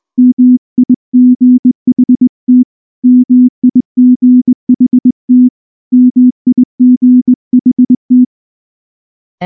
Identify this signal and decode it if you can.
Morse code, heard as a keyed tone, about 19 wpm, 250 Hz. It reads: MIGHT MIGHT MIGHT
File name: morse-training-20WPM